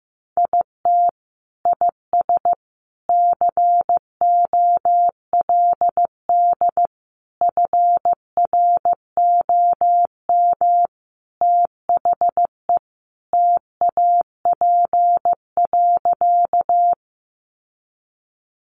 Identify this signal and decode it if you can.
Morse code, heard as a keyed tone, about 15 wpm, 700 Hz. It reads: IT IS COLD FROM THE TAP.
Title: Sentences from Top 1400 Words 15wpm